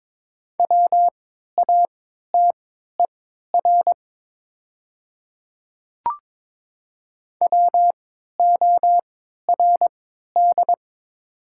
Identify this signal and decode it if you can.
Morse code, heard as a keyed tone, about 22 wpm, spaced as for 13 wpm, 700 Hz. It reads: WATER WORD
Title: 3x Character Spacing - Top 100 Words - 22wpm